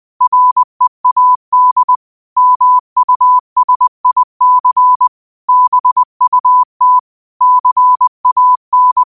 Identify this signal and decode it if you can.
Morse code, heard as a keyed tone, about 20 wpm, 1000 Hz. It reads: READ MUSIC BUT CAN